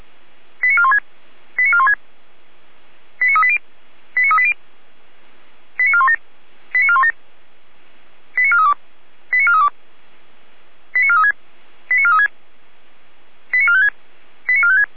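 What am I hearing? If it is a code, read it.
Five-tone selective call (ZVEI), heard as quick runs of five tones, tones of about 70 ms. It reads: 89416 89416 89380 89380 89417 89417 89432 89432 89436 89436 89456 89456
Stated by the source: ZVEI five-tone selective call received by SDR.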